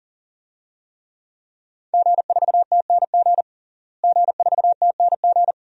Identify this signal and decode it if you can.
Morse code, heard as a keyed tone, about 40 wpm, 700 Hz. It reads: G4TNG G4TNG
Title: DXCC-100 Call Signs - 5 Characters - Repeated 2x - 40wpm